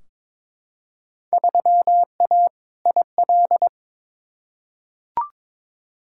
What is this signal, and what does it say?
Morse code, heard as a keyed tone, about 22 wpm, 700 Hz. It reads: 3A IL
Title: Field Day Exchanges - 22wpm